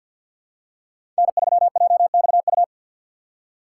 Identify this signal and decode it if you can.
Morse code, heard as a keyed tone, about 52 wpm, 700 Hz. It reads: N3JXU
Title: USA Call Signs - 5 Characters - 3x Character Spacing - 50wpm